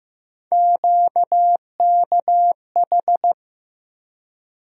Morse code, heard as a keyed tone, about 15 wpm, 700 Hz. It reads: QKH
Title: USA Call Sign Postfixes - 3 Letters 15wpm